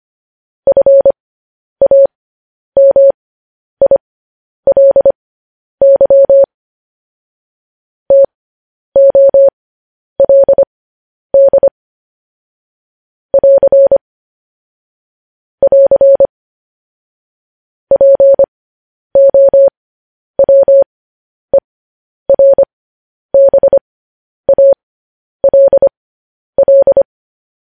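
Morse code, heard as a keyed tone, about 25 wpm, spaced as for 10 wpm, 550 Hz. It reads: FAMILY TOLD <AR> <AR> POWERBALL
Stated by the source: Morse code news headlines broadcast.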